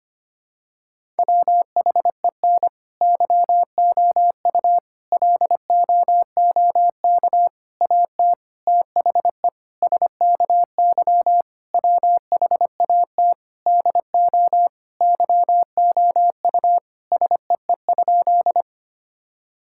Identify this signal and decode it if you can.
Morse code, heard as a keyed tone, about 25 wpm, 700 Hz. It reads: WHEN YOU LOOK AT THE SKY WHAT DO YOU SEE?
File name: practice / 25wpm / Sentences with 3-Word Phrases - 25wpm